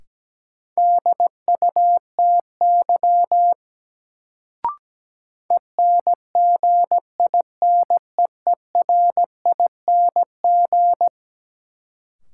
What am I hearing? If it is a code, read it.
Morse code, heard as a keyed tone, about 17 wpm, 700 Hz. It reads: DUTY ENGINEERING
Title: Sentences from Top 1500 Words - New Word Drill Alt - 17wpm